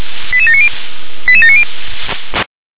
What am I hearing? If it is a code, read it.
Five-tone selective call (ZVEI), heard as quick runs of five tones, tones of about 70 ms. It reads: 88790 88790
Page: SDR/ZVEI